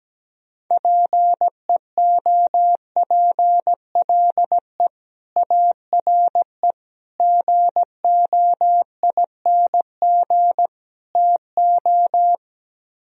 Morse code, heard as a keyed tone, about 17 wpm, 700 Hz. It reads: PEOPLE ARE GOING TO
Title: Speed-Racing - Sets of 4 Words 17wpm